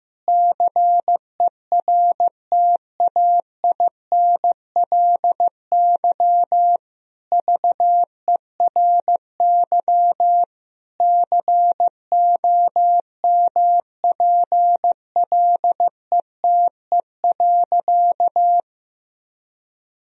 Morse code, heard as a keyed tone, about 15 wpm, 700 Hz. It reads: CERTAINLY VERY COMPLETE.
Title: Sentences from Top 1000 Words - Review Part 2 - 15wpm